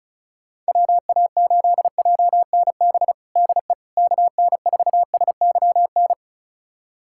Morse code, heard as a keyed tone, about 36 wpm, 700 Hz. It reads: WA8JNB DE KN4SYN
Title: USA Callsign DE Callsign - 6 Characters - 35wpm